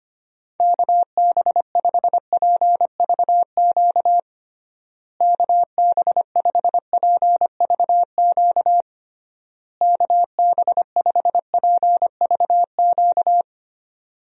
Morse code, heard as a keyed tone, about 25 wpm, 700 Hz. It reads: KB5PVQ KB5PVQ KB5PVQ
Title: USA Call Signs - 6 Characters - Repeated 3x - 25wpm